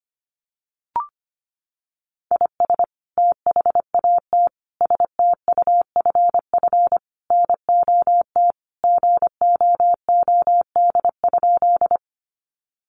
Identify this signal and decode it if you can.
Morse code, heard as a keyed tone, about 25 wpm, 700 Hz. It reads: IS THAT STUFF NOT GOOD?
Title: Sentences from Top 1400 Words 25wpm